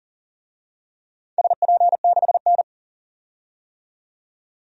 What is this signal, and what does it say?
Morse code, heard as a keyed tone, about 40 wpm, 700 Hz. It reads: SP6N
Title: DXCC-100 Call Signs - 4 Characters - 40wpm